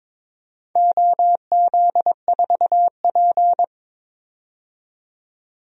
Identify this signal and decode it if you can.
Morse code, heard as a keyed tone, about 22 wpm, 700 Hz. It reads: OZ4P
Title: DXCC-100 Call Signs - 4 Characters - 22wpm